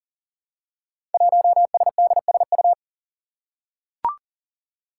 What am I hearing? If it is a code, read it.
Morse code, heard as a keyed tone, about 42 wpm, 700 Hz. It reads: 1SDSU
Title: Five Challenging Letter-Number Combinations 40wpm